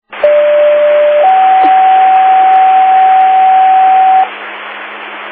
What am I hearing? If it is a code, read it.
Two-tone page, heard as a one-second tone, then a three-second tone, about 590 then 760 Hz, with noise. It is heard from 0.2 to 4.2 s.